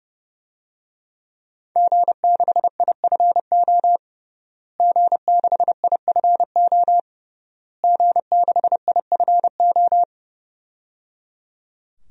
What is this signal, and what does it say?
Morse code, heard as a keyed tone, about 29 wpm, 700 Hz. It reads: G6IFO G6IFO G6IFO
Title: DXCC-100 Call Signs - 5 Characters - Repeated 3x - 30wpm